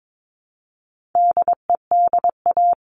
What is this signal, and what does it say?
Morse code, heard as a keyed tone, about 22 wpm, 700 Hz. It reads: DEDA
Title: All Cut Numbers - 4-Digit 22wpm